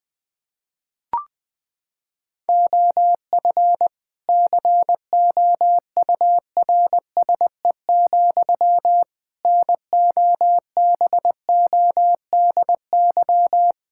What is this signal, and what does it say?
Morse code, heard as a keyed tone, about 20 wpm, 700 Hz. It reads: OF COURSE, NOBODY
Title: Sentences from Top 1600 Words 20wpm